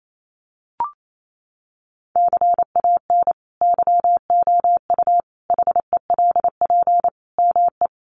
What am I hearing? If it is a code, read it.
Morse code, heard as a keyed tone, about 28 wpm, 700 Hz. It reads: CAN YOU HELP ME
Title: Sentences from Top 200 Words 28wpm